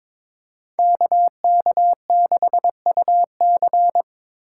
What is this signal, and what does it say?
Morse code, heard as a keyed tone, about 26 wpm, 700 Hz. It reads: KK6UC
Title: Speed Racing - USA 2x2 Call Signs 22wpm